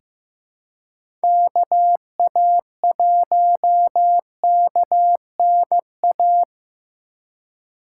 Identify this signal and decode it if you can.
Morse code, heard as a keyed tone, about 15 wpm, 700 Hz. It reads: KA1KNA
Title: USA Call Signs - 6 Characters - 3x Character Spacing - 15wpm